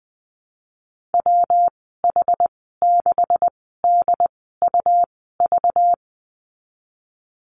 Morse code, heard as a keyed tone, about 20 wpm, spaced as for 15 wpm, 700 Hz. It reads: WH6DUV
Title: 238 Lesson 38g - Letter q - Call Signs 20wpm